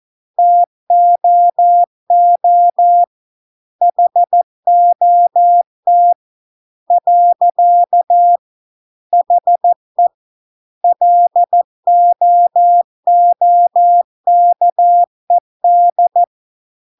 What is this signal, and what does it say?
Morse code, heard as a keyed tone, about 14 wpm, 700 Hz. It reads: TOO HOT . HE LOOKED